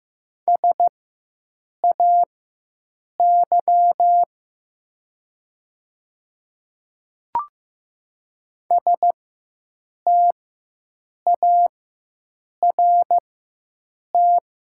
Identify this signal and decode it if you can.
Morse code, heard as a keyed tone, about 15 wpm, spaced as for 7 wpm, 700 Hz. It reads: SAY START
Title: 4x Character Spacing - Top 300 Words - 15wpm